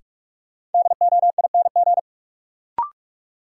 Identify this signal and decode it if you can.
Morse code, heard as a keyed tone, about 46 wpm, 700 Hz. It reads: DOING
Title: Sets of 4 Words - New Word Drill - 45wpm